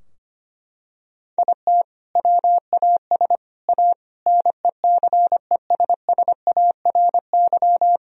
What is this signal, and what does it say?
Morse code, heard as a keyed tone, about 25 wpm, 700 Hz. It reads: IT WAS A NECESSARY
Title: Sentences from Top 1000 Words - Review Part 2 - 25wpm